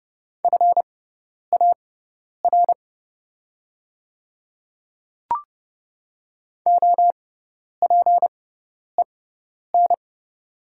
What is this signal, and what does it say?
Morse code, heard as a keyed tone, about 29 wpm, spaced as for 10 wpm, 700 Hz. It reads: FAR OPEN
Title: 6x Character Spacing - Top 300 Words - 30wpm